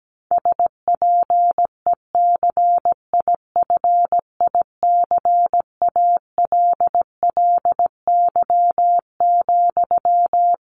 Morse code, heard as a keyed tone, about 17 wpm, 700 Hz. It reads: SPECIFICALLY,
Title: Sentences from Top 1100 Words 17wpm